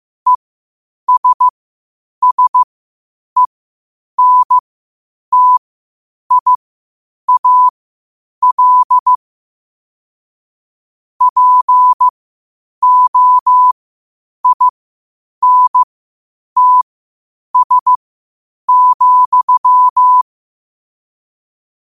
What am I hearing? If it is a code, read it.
Morse code, heard as a keyed tone, about 15 wpm, spaced as for 8 wpm, 1000 Hz. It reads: ESSENTIAL POINTS,
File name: qotdfiles / 8wpm